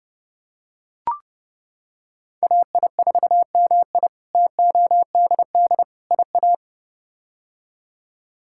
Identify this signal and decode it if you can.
Morse code, heard as a keyed tone, about 30 wpm, 700 Hz. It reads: AI4MI TODD IA
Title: K1USN SST QSO Practice - 30wpm